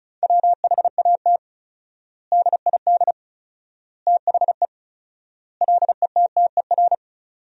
Morse code, heard as a keyed tone, about 36 wpm, 700 Hz. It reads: WHAT DID THE LETTER
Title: 4x Word-Spacing - Sentences from Top 500 Words - 35wpm